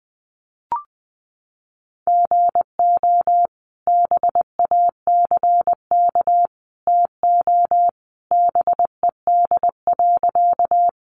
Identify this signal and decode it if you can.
Morse code, heard as a keyed tone, about 20 wpm, 700 Hz. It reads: GO BACK TO BED.
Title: Sentences from Top 500 Words 20wpm